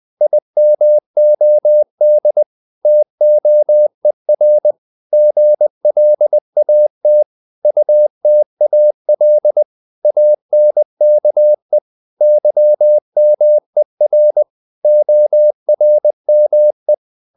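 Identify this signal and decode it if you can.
Morse code, heard as a keyed tone, about 20 wpm, 600 Hz. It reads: IMOD TOER GLAT UTAL ANKE YMER ORME
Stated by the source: Morse code practice of short Danish words at 20 wpm, from CW med Gnister.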